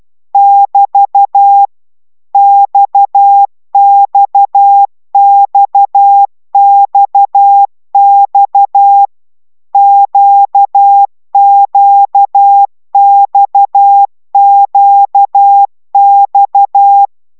Morse code, heard as a keyed tone, about 12 wpm, 800 Hz. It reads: <BT> XXXXX QQXQX
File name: taviro tanulas / cw 69 16 10